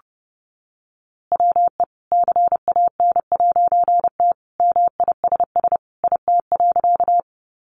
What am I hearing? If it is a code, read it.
Morse code, heard as a keyed tone, about 30 wpm, 700 Hz. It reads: WE CAN'T MISS IT.
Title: A Sound Of Thunder 30wpm